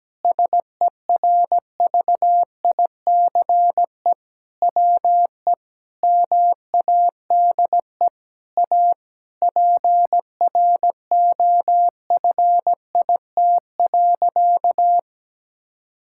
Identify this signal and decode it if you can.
Morse code, heard as a keyed tone, about 17 wpm, 700 Hz. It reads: SERVICE WE MADE A PROFIT.
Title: Sentences from Top 1000 Words - Review Part 5 - 17wpm